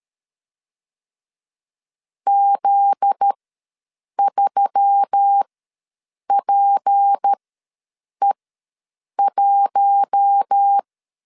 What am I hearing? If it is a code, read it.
Morse code, heard as a keyed tone, about 13 wpm, spaced as for 7 wpm, 800 Hz. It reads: Z3PE1
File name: LEZIONI / 8 YJ45 / 8 YJ4 ws 12wpm 3249